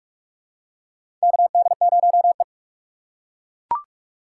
Morse code, heard as a keyed tone, about 41 wpm, 700 Hz. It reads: KD0E
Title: USA Call Signs - 6x Character Spacing - 45wpm